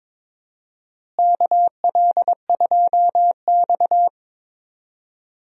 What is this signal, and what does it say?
Morse code, heard as a keyed tone, about 22 wpm, 700 Hz. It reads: KL2X
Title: USA Call Signs - 4 Characters - Repeated 3x - 22wpm